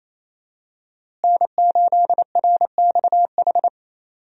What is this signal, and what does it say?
Morse code, heard as a keyed tone, about 28 wpm, 700 Hz. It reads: N8RXH
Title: USA Call Signs - 5 Characters - Repeated 3x - 28wpm